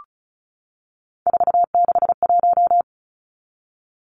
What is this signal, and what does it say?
Morse code, heard as a keyed tone, about 34 wpm, 700 Hz. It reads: 461
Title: Numbers - 3-Digit 35wpm